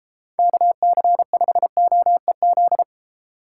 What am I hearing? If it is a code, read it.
Morse code, heard as a keyed tone, about 32 wpm, 700 Hz. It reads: KC5OEZ